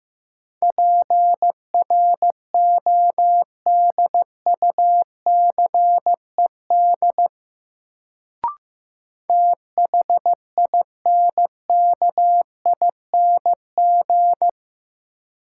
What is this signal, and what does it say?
Morse code, heard as a keyed tone, about 15 wpm, 700 Hz. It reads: PRODUCED THINKING
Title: Sentences from Top 1000 Words - Review Part 6 - New Word Drill - 15wpm